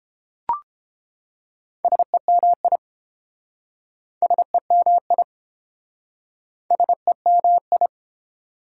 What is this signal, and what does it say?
Morse code, heard as a keyed tone, about 32 wpm, 700 Hz. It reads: SEMI SEMI SEMI